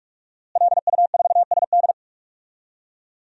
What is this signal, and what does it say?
Morse code, heard as a keyed tone, about 41 wpm, 700 Hz. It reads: LU4SD
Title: DXCC-100 Call Signs - 5 Characters - Repeated 2x - 45wpm